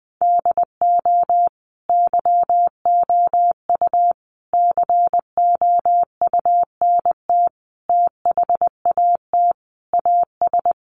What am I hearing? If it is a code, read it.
Morse code, heard as a keyed tone, about 20 wpm, 700 Hz. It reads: DO YOU COUNT THAT AS
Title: Sentences from Top 1500 Words 20wpm